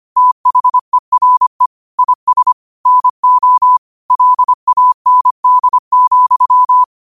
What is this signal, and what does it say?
Morse code, heard as a keyed tone, about 25 wpm, 1000 Hz. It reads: THERE IS NO LAND,
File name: qotdfiles / 25wpm